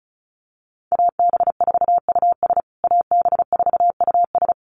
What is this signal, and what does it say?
Morse code, heard as a keyed tone, about 34 wpm, 700 Hz. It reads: AB4US AB4US
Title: USA Call Signs Repeated 2x 35wpm